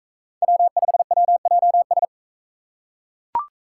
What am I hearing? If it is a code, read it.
Morse code, heard as a keyed tone, about 41 wpm, 700 Hz. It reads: W5WJS